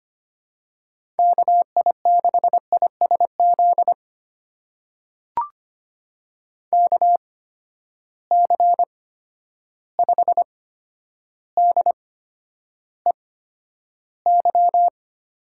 Morse code, heard as a keyed tone, about 25 wpm, spaced as for 11 wpm, 700 Hz. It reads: KI6ISZ KC5DEY
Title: USA Call Signs - 6 Characters - 8x Character Spacing - 25wpm